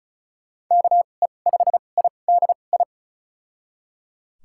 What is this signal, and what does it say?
Morse code, heard as a keyed tone, about 34 wpm, spaced as for 25 wpm, 700 Hz. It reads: KE5IDI
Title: USA Call Signs - 2x Character Spacing - 35wpm